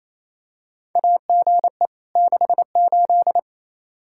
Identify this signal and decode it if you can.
Morse code, heard as a keyed tone, about 28 wpm, 700 Hz. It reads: AGE 68
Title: Practice QSO - Exchange 3 - More Info - 28wpm